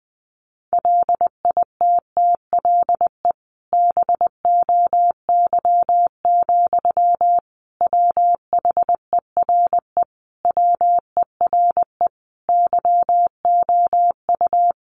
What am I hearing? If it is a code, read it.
Morse code, heard as a keyed tone, about 20 wpm, 700 Hz. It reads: LITTLE BOY, WHERE WERE YOU
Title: Sentences from Top 1000 Words 20wpm